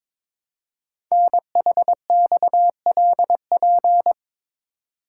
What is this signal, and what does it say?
Morse code, heard as a keyed tone, about 22 wpm, 700 Hz. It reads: NHXLP
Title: Code Groups - 5 Letter:Numbers - 22wpm